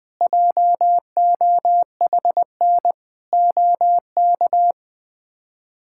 Morse code, heard as a keyed tone, about 20 wpm, 700 Hz. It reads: JOHN OK
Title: K1USN SST QSO Practice - 20wpm